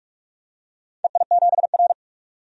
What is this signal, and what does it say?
Morse code, heard as a keyed tone, about 44 wpm, 700 Hz. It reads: EI7R
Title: DXCC-100 Call Signs - 4 Characters - 45wpm